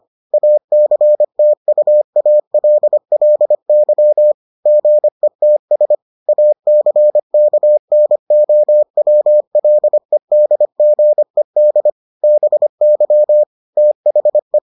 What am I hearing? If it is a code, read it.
Morse code, heard as a keyed tone, about 25 wpm, 600 Hz. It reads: ACTUALLY GETS ACKNOWLEDGED BY THE